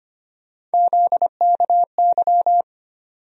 Morse code, heard as a keyed tone, about 25 wpm, 700 Hz. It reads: ZKY
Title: USA Call Sign Postfixes - 3 Letters 25wpm